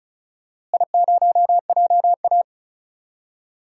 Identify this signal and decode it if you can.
Morse code, heard as a keyed tone, about 36 wpm, 700 Hz. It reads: I0JA
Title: DXCC-100 Call Signs - 4 Characters - Repeated 2x - 35wpm